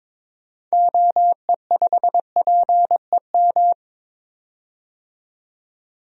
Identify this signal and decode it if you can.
Morse code, heard as a keyed tone, about 21 wpm, 700 Hz. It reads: OE5PEM
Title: DXCC-100 Call Signs - 6 Characters - 22wpm